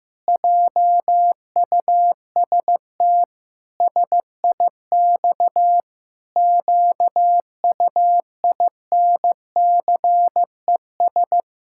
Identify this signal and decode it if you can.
Morse code, heard as a keyed tone, about 15 wpm, 700 Hz. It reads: JUST SIX QUINCES
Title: Pangram Challenge v2 - 15wpm